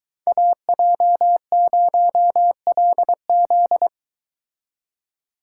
Morse code, heard as a keyed tone, about 22 wpm, 700 Hz. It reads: AJ0LZ